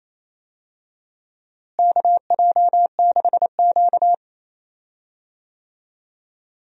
Morse code, heard as a keyed tone, about 28 wpm, 700 Hz. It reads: KJ6Q
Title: USA Call Signs - 4 Characters - Repeated 3x - 28wpm